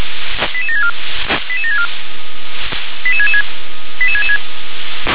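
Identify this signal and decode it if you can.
Five-tone selective call (ZVEI), heard as quick runs of five tones, tones of about 70 ms. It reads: 88764 88764 88666 88666
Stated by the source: ZVEI five-tone call received by SDR.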